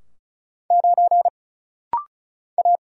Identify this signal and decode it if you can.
Morse code, heard as a keyed tone, about 34 wpm, 700 Hz. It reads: 9 A
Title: Single Letter-Number Alt - Rapid-Fire 35wpm